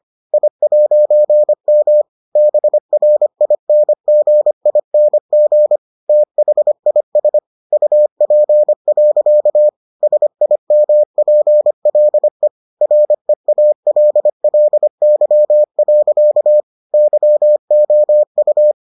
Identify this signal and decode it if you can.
Morse code, heard as a keyed tone, about 25 wpm, 600 Hz. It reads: I'M BRINGING THIS UP. SIMPLE REALLY. YOU